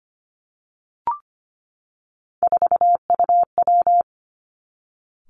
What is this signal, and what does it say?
Morse code, heard as a keyed tone, about 25 wpm, 700 Hz. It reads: 4UW
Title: Three Challenging Letter-Number Combinations 25wpm